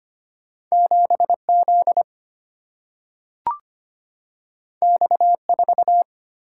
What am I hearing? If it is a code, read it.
Morse code, heard as a keyed tone, about 25 wpm, 700 Hz. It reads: 7 Z X 4
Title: Single-Letter Single-Number 25wpm